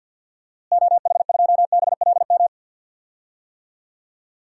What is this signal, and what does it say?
Morse code, heard as a keyed tone, about 52 wpm, 700 Hz. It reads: OH2BLM
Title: DXCC-100 Call Signs - 6 Characters - Repeated 2x - 50wpm